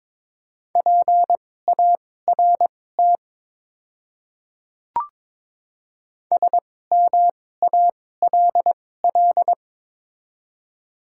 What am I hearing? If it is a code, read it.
Morse code, heard as a keyed tone, about 22 wpm, spaced as for 16 wpm, 700 Hz. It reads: PART SMALL
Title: 2x Character Spacing - Top 200 Words - 22wpm